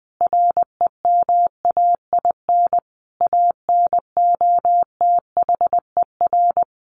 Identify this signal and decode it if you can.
Morse code, heard as a keyed tone, about 20 wpm, 700 Hz. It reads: REMAIN ANOTHER
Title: Sentences from Top 1200 Words 20wpm